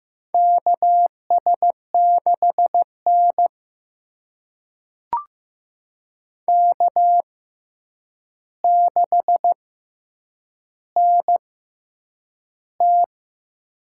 Morse code, heard as a keyed tone, about 15 wpm, spaced as for 8 wpm, 700 Hz. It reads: KS6N K6NT
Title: USA Call Signs - 4 Characters - 6x Character Spacing - 15wpm